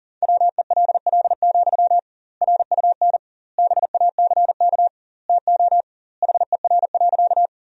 Morse code, heard as a keyed tone, about 37 wpm, 700 Hz. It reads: WELL, RUN BACK TO HER.
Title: 2x Word-Spacing - Sentences from Top 300 Words - 40wpm